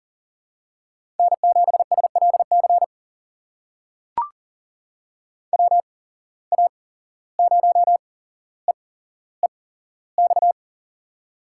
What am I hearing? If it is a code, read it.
Morse code, heard as a keyed tone, about 40 wpm, spaced as for 16 wpm, 700 Hz. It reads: N7SLC WA0EEX
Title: USA Call Signs - 8x Character Spacing - 40wpm